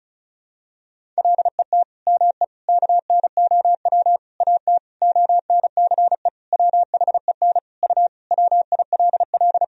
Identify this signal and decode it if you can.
Morse code, heard as a keyed tone, about 36 wpm, 700 Hz. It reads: LET ME KNOW AT ONCE WHEN U WILL
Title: Sentences from Top 300 Words and Abbreviations 35wpm